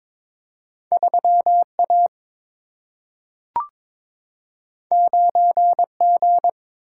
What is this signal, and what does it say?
Morse code, heard as a keyed tone, about 22 wpm, 700 Hz. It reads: 3 A 9 G
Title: Single-Letter Single-Number 22wpm